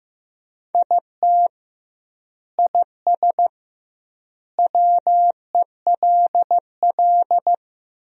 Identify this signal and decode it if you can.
Morse code, heard as a keyed tone, about 15 wpm, 700 Hz. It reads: IT IS WELL
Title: 2x Word-Spacing - Sentences from Top 300 Words - 15wpm